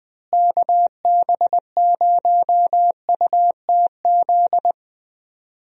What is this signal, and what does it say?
Morse code, heard as a keyed tone, about 21 wpm, 700 Hz. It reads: KB0UTZ
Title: USA Call Signs - 6 Characters - Repeated 3x - 20wpm